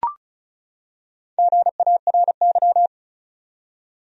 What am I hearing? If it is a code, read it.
Morse code, heard as a keyed tone, about 34 wpm, 700 Hz. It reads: GARY
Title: USA Common Names - Max 4 Letters - 35wpm